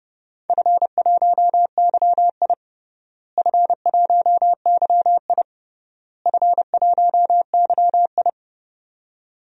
Morse code, heard as a keyed tone, about 30 wpm, 700 Hz. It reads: F1YI F1YI F1YI
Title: DXCC-100 Call Signs - 4 Characters - Repeated 3x - 30wpm